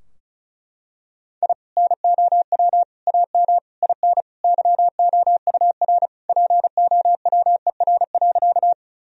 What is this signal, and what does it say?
Morse code, heard as a keyed tone, about 34 wpm, 700 Hz. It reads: I NOW AM IN YOUR POWER.
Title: Sentences from Top 1000 Words - Review Part 2 - 35wpm